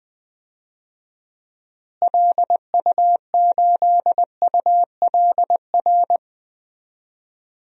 Morse code, heard as a keyed tone, about 20 wpm, 700 Hz. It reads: LU8ULR
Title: DXCC-100 Call Signs - 6 Characters - Repeated 3x - 20wpm